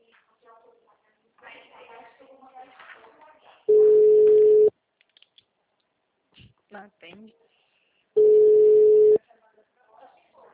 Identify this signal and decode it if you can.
Ringback tone (Europe, 425 Hz), with speech.
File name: out- -2016- - - .56075